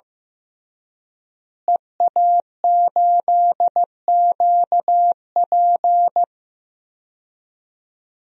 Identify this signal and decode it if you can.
Morse code, heard as a keyed tone, about 15 wpm, 700 Hz. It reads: EA8QP
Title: DXCC-100 Call Signs - 5 Characters - Repeated 3x - 15wpm